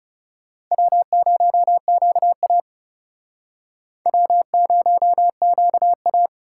Morse code, heard as a keyed tone, about 37 wpm, 700 Hz. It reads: W0QA W0QA
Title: Speed Racing - USA 1x2 Call Signs 30wpm